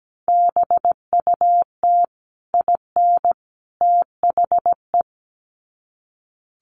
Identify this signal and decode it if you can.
Morse code, heard as a keyed tone, about 17 wpm, 700 Hz. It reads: BUT IN THE
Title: Sets of 3 Words 17wpm